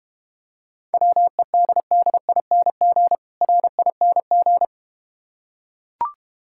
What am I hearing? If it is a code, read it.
Morse code, heard as a keyed tone, about 30 wpm, 700 Hz. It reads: WEDDING RING
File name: practice / 25wpm / Difficult Words - Speed-Racing with Hints - 25wpm